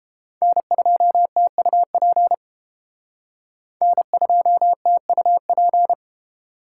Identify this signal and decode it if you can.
Morse code, heard as a keyed tone, about 28 wpm, 700 Hz. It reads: N2TUP N2TUP